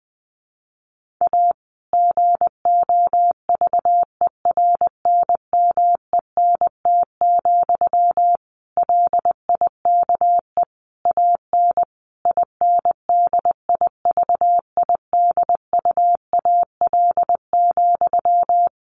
Morse code, heard as a keyed tone, about 19 wpm, 700 Hz. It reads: A GOVERNMENT, LIKE AN INDIVIDUAL,
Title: Sentences from Top 1400 Words 20wpm